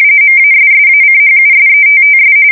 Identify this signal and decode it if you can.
RTTY, about 45.45 baud. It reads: MASSACHUSETTS
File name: fsk170 / USS-Massachusetts